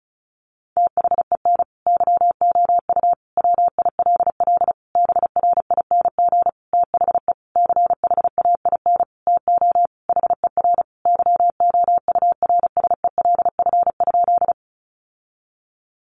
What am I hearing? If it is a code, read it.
Morse code, heard as a keyed tone, about 34 wpm, 700 Hz. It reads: THEN YOU WILL BRING THE CHAIN TO HER YOURSELF?
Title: Sentences from Top 1500 Words 35wpm